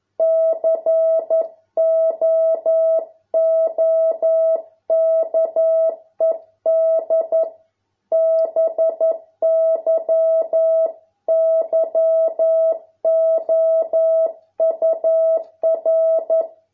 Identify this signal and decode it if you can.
Morse code, heard as a keyed tone, about 11 wpm, 650 Hz. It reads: COOKED BY YOUR